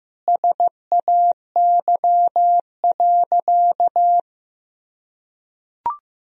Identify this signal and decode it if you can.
Morse code, heard as a keyed tone, about 15 wpm, 700 Hz. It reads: SAY.
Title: Sentences from Top 1000 Words - Review Part 2 - 15wpm